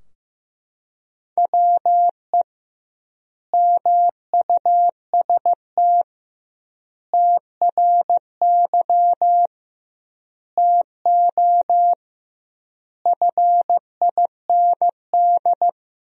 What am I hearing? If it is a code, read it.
Morse code, heard as a keyed tone, about 16 wpm, 700 Hz. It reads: WE MUST TRY TO FIND
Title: 2x Word-Spacing - Sentences from Top 400 Words - 15wpm